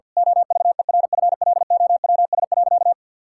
Morse code, heard as a keyed tone, about 50 wpm, 700 Hz. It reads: OVERFLOWS.
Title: Pangram Challenge v2 - 50wpm